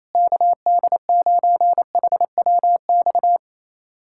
Morse code, heard as a keyed tone, about 28 wpm, 700 Hz. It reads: KD9HWX
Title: USA Callsign DE Callsign - 6 Characters - 28wpm